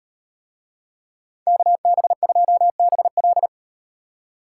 Morse code, heard as a keyed tone, about 38 wpm, 700 Hz. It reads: KB2BL